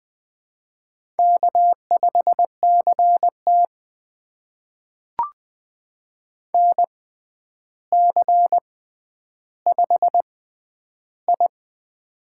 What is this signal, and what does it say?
Morse code, heard as a keyed tone, about 20 wpm, spaced as for 10 wpm, 700 Hz. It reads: K5CT NC5I
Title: USA Call Signs - 4 Characters - 6x Character Spacing - 20wpm